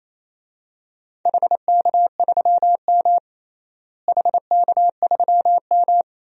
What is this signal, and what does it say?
Morse code, heard as a keyed tone, about 28 wpm, 700 Hz. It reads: HK3M HK3M
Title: DXCC-100 Call Signs - 4 Characters - Repeated 3x - 28wpm